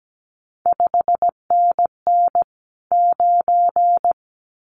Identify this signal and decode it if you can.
Morse code, heard as a keyed tone, about 17 wpm, 700 Hz. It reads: 5NN 9
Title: CQ WW DX Contest Exchange - 17wpm